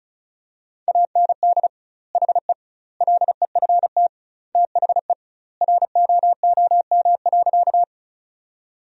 Morse code, heard as a keyed tone, about 36 wpm, 700 Hz. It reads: AND HE LEFT THE ROOM.
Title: 2x Word-Spacing - Sentences from Top 300 Words - 35wpm